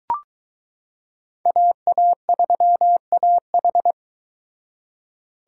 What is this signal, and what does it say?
Morse code, heard as a keyed tone, about 23 wpm, 700 Hz. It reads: AA3AH